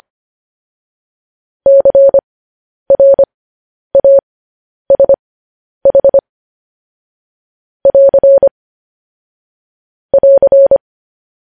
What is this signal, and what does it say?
Morse code, heard as a keyed tone, about 25 wpm, spaced as for 10 wpm, 550 Hz. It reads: CRASH <AR> <AR>
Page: news headlines in morse code at 10 wpm